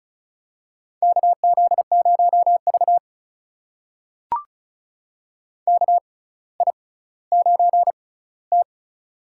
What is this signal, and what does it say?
Morse code, heard as a keyed tone, about 36 wpm, spaced as for 18 wpm, 700 Hz. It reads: KZ0V KI9T
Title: USA Call Signs - 4 Characters - 6x Character Spacing - 35wpm